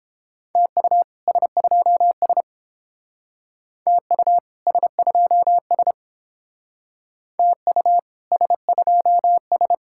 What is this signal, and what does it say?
Morse code, heard as a keyed tone, about 28 wpm, 700 Hz. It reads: TU S2S TU S2S TU S2S